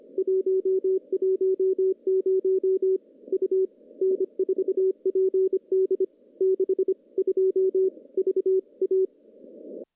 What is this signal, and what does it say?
Morse code, heard as a keyed tone, about 27 wpm, 380 Hz. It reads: 110 U N4PD 62 VA
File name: contests / CW-SS-k5tr- -0200